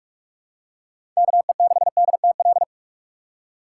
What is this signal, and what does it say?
Morse code, heard as a keyed tone, about 46 wpm, 700 Hz. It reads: KE6DTL
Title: USA Call Signs - 6 Characters - 3x Character Spacing - 45wpm